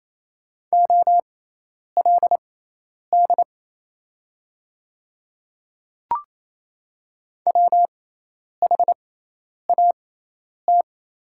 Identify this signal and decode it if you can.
Morse code, heard as a keyed tone, about 27 wpm, spaced as for 10 wpm, 700 Hz. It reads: OLD WHAT
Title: 6x Character Spacing - Top 300 Words - 28wpm